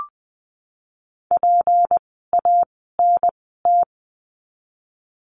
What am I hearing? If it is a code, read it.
Morse code, heard as a keyed tone, about 20 wpm, spaced as for 14 wpm, 700 Hz. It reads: PANT
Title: 153 Lesson 25f - Letter p - Words 20wpm